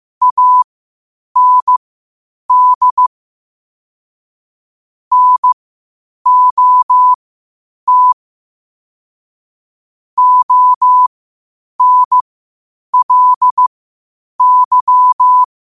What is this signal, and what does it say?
Morse code, heard as a keyed tone, about 15 wpm, spaced as for 8 wpm, 1000 Hz. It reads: AND NOT ONLY
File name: qotdfiles / 8wpm